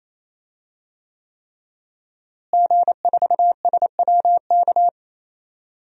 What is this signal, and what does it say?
Morse code, heard as a keyed tone, about 28 wpm, 700 Hz. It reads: G4SWK
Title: DXCC-100 Call Signs - 5 Characters - Repeated 3x - 28wpm